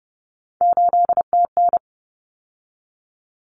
Morse code, heard as a keyed tone, about 30 wpm, 700 Hz. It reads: 8TN
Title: Common Cut Numbers - 3-Digit 30wpm